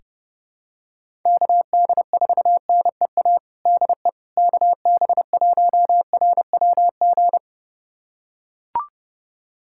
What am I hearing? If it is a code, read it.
Morse code, heard as a keyed tone, about 30 wpm, 700 Hz. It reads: KD4NEA DE KB1RWG
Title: USA Callsign DE Callsign - 6 Characters - 30wpm